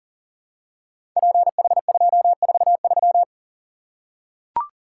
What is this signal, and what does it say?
Morse code, heard as a keyed tone, about 42 wpm, 700 Hz. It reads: PH243
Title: Code Groups - 5 Letter:Numbers - 40wpm